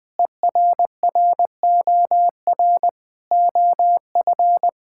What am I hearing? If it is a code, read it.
Morse code, heard as a keyed tone, about 21 wpm, 700 Hz. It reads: ERROR OF
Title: Sets of 6 Words - 20wpm